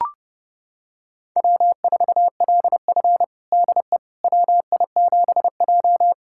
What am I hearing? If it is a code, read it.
Morse code, heard as a keyed tone, about 30 wpm, 700 Hz. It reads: W4LF DE WI7J
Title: USA Callsign DE Callsign - 4 Characters - 30wpm